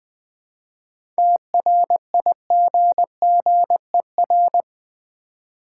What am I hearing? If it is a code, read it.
Morse code, heard as a keyed tone, about 20 wpm, 700 Hz. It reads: TRIGGER
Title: 7 Letter Words - First 3 Letters Given - 20wpm